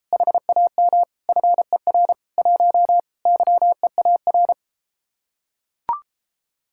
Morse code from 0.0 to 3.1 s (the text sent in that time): HAM FER 1